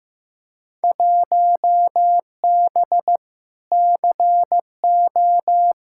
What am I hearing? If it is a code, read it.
Morse code, heard as a keyed tone, about 15 wpm, 700 Hz. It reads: 1B CO
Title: Field Day Exchanges - 15wpm